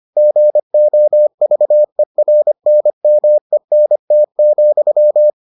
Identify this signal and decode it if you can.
Morse code, heard as a keyed tone, about 25 wpm, 600 Hz. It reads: GOVERNMENT,